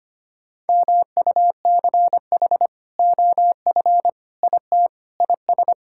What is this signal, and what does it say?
Morse code, heard as a keyed tone, about 25 wpm, 700 Hz. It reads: MUCH OF IT IS
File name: practice / 25wpm / Sentences with 2-Word Phrases - 25wpm